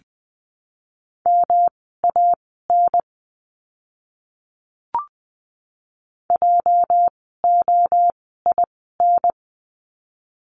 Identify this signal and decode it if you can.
Morse code, heard as a keyed tone, about 20 wpm, spaced as for 15 wpm, 700 Hz. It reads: MAN JOIN
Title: 206 Lesson 33f - Letter j - Words 20wpm